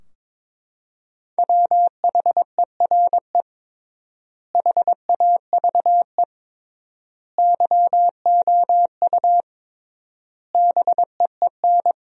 Morse code, heard as a keyed tone, about 22 wpm, 700 Hz. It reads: WHERE HAVE YOU BEEN
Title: 3x Word-Spacing - Sentences from Top 300 Words - 22wpm